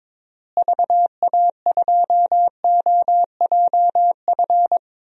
Morse code, heard as a keyed tone, about 22 wpm, 700 Hz. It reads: VA2OJF
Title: DXCC-100 Call Signs - 6 Characters - Repeated 3x - 22wpm